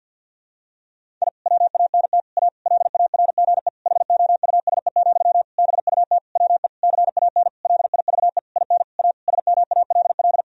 Morse code, heard as a keyed tone, about 50 wpm, 700 Hz. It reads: I WANT A LARGE HOUSE, BUT WE CAN LIVE IN A SMALL
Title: Sentences from Top 1000 Words - Review Part 1 - 50wpm